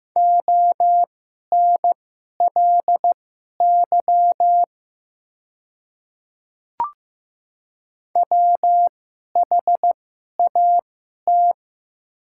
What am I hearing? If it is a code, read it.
Morse code, heard as a keyed tone, about 15 wpm, spaced as for 11 wpm, 700 Hz. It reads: ONLY WHAT
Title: 2x Character Spacing - Top 300 Words - 15wpm